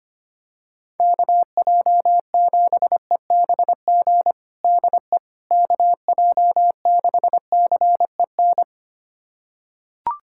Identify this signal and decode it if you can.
Morse code, heard as a keyed tone, about 25 wpm, 700 Hz. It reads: KJ7EBG DE KJ6CEN
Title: USA Callsign DE Callsign - 6 Characters - 25wpm